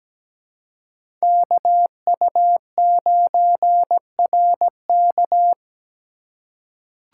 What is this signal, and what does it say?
Morse code, heard as a keyed tone, about 17 wpm, 700 Hz. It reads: KU9RK
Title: K1USN SST QSO Practice - 17wpm